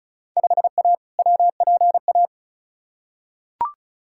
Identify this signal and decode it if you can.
Morse code, heard as a keyed tone, about 34 wpm, 700 Hz. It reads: 5A WPA
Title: Field Day Exchanges Encore - 35wpm